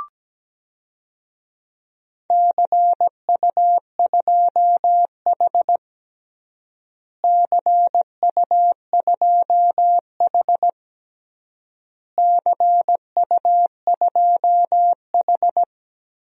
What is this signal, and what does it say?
Morse code, heard as a keyed tone, about 17 wpm, 700 Hz. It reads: CU2H CU2H CU2H
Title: DXCC-100 Call Signs - 4 Characters - Repeated 3x - 17wpm